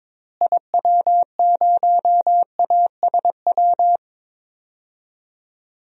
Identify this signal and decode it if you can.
Morse code, heard as a keyed tone, about 22 wpm, 700 Hz. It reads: IW0ASW
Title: DXCC-100 Call Signs - 6 Characters - Repeated 2x - 22wpm